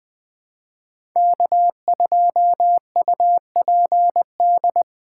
Morse code, heard as a keyed tone, about 21 wpm, 700 Hz. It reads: K2UPD
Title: 2023 POTA Hunter Call Signs - 20wpm